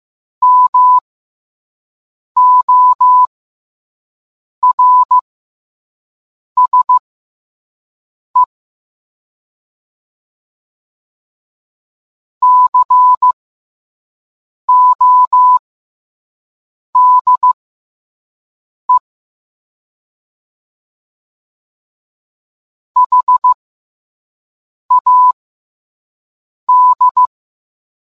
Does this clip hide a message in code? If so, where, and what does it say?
Yes: Morse code from 0.4 to 27.3 s, heard as a keyed tone, about 15 wpm, spaced as for 5 wpm, 1000 Hz. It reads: MORSE CODE HAD